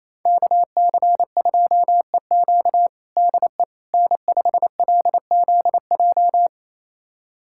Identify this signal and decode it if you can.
Morse code, heard as a keyed tone, about 28 wpm, 700 Hz. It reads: KC2EQ DE N5LZJ
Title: USA Callsign DE Callsign - 5 Characters - 28wpm